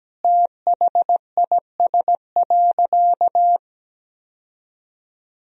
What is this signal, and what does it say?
Morse code, heard as a keyed tone, about 17 wpm, 700 Hz. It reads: THIS.
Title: Sentences from Top 1000 Words - Review Part 1 - 17wpm